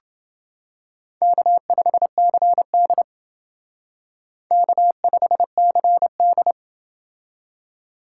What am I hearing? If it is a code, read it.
Morse code, heard as a keyed tone, about 29 wpm, 700 Hz. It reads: K5CD K5CD